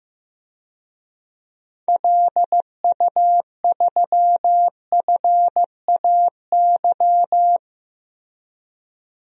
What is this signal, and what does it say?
Morse code, heard as a keyed tone, about 15 wpm, 700 Hz. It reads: LU3FAY
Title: DXCC-100 Call Signs - 6 Characters - 15wpm